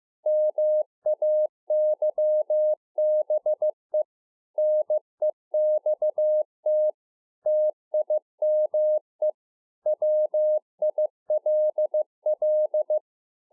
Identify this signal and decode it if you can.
Morse code, heard as a keyed tone, about 15 wpm, 610 Hz. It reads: MAYBE NEXT TIME WILL